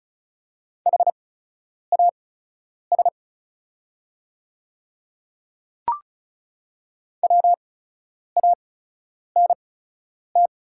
Morse code, heard as a keyed tone, about 36 wpm, spaced as for 10 wpm, 700 Hz. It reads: HAS WANT
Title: 8x Character Spacing - Top 200 Words - 35wpm